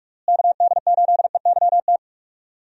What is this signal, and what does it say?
Morse code, heard as a keyed tone, about 42 wpm, 700 Hz. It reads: KD8EYT
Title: USA Call Signs - 6 Characters - Repeated 3x - 45wpm